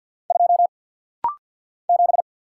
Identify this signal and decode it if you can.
Morse code, heard as a keyed tone, about 49 wpm, 700 Hz. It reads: 2 7
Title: Single Letter-Number - Rapid-Fire 50wpm